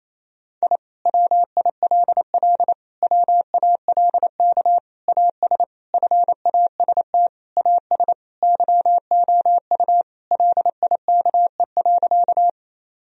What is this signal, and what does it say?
Morse code, heard as a keyed tone, about 28 wpm, 700 Hz. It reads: I WILL WALK AS FAST AS YOU LIKE.
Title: Sentences from Top 1000 Words - Review Part 2 - 28wpm